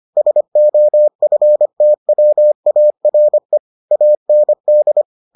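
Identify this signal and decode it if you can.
Morse code, heard as a keyed tone, about 25 wpm, 600 Hz. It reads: SOFTWARE AND
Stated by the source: Morse edition of the Foundations of Amateur Radio episodes.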